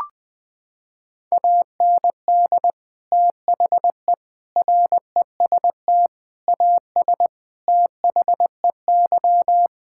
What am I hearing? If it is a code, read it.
Morse code, heard as a keyed tone, about 21 wpm, 700 Hz. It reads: AND THE REST AS THEY
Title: Sentences with 8-Word Phrases - 20wpm